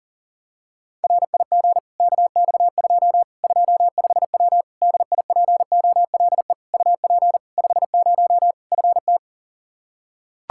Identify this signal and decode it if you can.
Morse code, heard as a keyed tone, about 40 wpm, 700 Hz. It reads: RIG KX2 25W DIPOLE UP 50 FT
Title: Practice QSO - Exchange 2 - Info - 40wpm